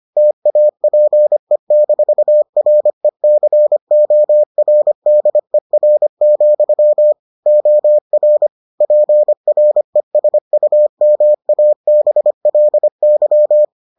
Morse code, heard as a keyed tone, about 25 wpm, 600 Hz. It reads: TAPE-RECORDER, OR PRESUMABLY